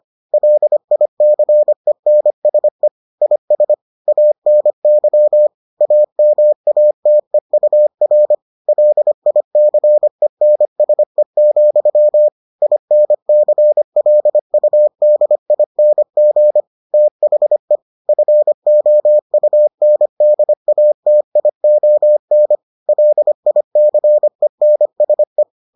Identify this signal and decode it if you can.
Morse code, heard as a keyed tone, about 25 wpm, 600 Hz. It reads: LICENSE IS ANY AMATEUR LICENSE, INCLUDING THE FOUNDATION LICENSE